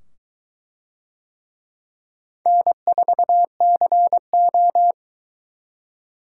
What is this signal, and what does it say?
Morse code, heard as a keyed tone, about 23 wpm, 700 Hz. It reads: N4CO